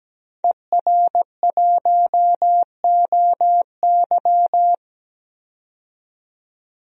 Morse code, heard as a keyed tone, about 17 wpm, 700 Hz. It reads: ER1OY
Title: DXCC-100 Call Signs - 5 Characters - 17wpm